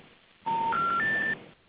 Special information tone, with noise.